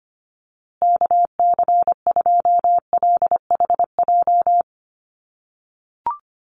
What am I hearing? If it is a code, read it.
Morse code, heard as a keyed tone, about 25 wpm, 700 Hz. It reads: KC2LHJ
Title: USA Call Signs Repeated 2x 25wpm